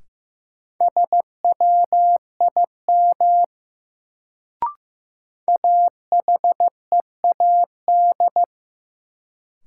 Morse code, heard as a keyed tone, about 15 wpm, 700 Hz. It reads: SWIM AHEAD
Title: Sentences from Top 1000 Words - Review Part 6 - New Word Drill - 15wpm